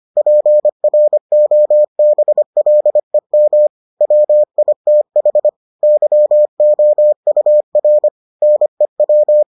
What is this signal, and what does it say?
Morse code, heard as a keyed tone, about 25 wpm, 600 Hz. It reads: PROBLEM WITH YOUR NEW